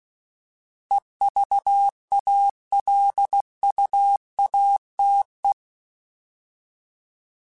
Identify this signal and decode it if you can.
Morse code, heard as a keyed tone, about 16 wpm, 800 Hz. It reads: EVALUATE
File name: Alphabetical / Alphabetical mp3s / Es @15